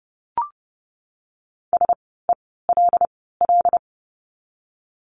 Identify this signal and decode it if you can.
Morse code, heard as a keyed tone, about 30 wpm, 700 Hz. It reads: S E L L
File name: Lesson 16f - Letter c - Words 30wpm